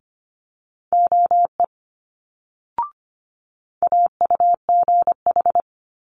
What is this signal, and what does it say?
Morse code, heard as a keyed tone, about 25 wpm, 700 Hz. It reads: OE AUGH
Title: Multi-letter Phonemes Quick Fire 25wpm